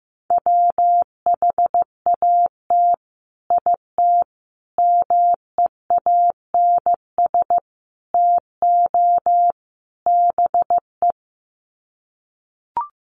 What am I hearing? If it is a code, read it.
Morse code, heard as a keyed tone, about 15 wpm, 700 Hz. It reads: WHAT IT MEANS TO BE
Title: Sets of 5 Words 15wpm